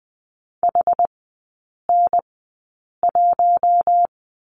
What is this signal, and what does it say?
Morse code, heard as a keyed tone, about 20 wpm, spaced as for 8 wpm, 700 Hz. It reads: HN1
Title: 081 Lesson 14e - Number 2 - Three Characters 20wpm